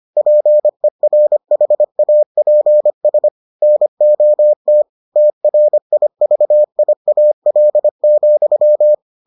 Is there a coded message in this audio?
Yes: Morse code from 0.2 to 8.9 s, heard as a keyed tone, about 25 wpm, 600 Hz. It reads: PERHAPS NOT TRIVIAL,